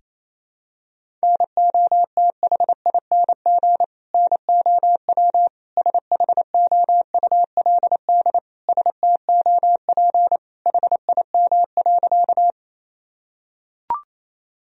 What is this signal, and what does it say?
Morse code, heard as a keyed tone, about 28 wpm, 700 Hz. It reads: NOTHING NOW SHOULD STOP HIM.
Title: Sentences from Top 1000 Words - Review Part 1 - 28wpm